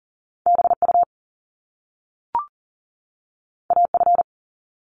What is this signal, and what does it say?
Morse code, heard as a keyed tone, about 40 wpm, 700 Hz. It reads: BU AF
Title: Top Two Letter Combinations in Words 40wpm